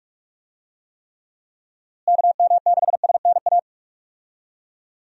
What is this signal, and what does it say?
Morse code, heard as a keyed tone, about 45 wpm, 700 Hz. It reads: KM6SNA
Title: USA Call Signs - 6 Characters - Repeated 3x - 45wpm